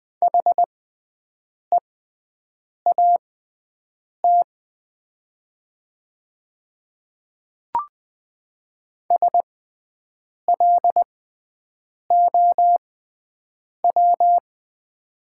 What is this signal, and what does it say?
Morse code, heard as a keyed tone, about 20 wpm, spaced as for 7 wpm, 700 Hz. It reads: HEAT SLOW
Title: 6x Character Spacing - Top 500 Words - 20wpm